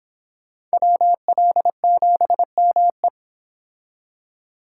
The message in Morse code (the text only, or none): WL7ME